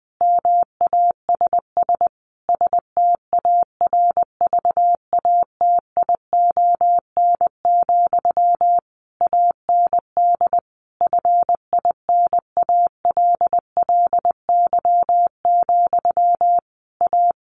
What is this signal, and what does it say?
Morse code, heard as a keyed tone, about 20 wpm, 700 Hz. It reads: MASS STARVATION, AND FINALLY, A
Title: A Sound Of Thunder 20wpm